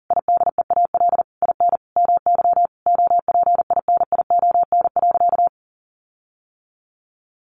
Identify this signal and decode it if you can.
Morse code, heard as a keyed tone, about 37 wpm, 700 Hz. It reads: IDEAL IN MY OPINION.
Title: Sentences from Top 1400 Words 40wpm